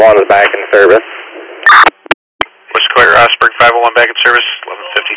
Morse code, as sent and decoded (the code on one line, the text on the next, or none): none